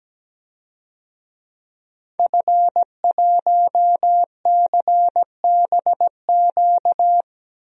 Morse code, heard as a keyed tone, about 17 wpm, 700 Hz. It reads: F1CBQ
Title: DXCC-100 Call Signs - 5 Characters - 17wpm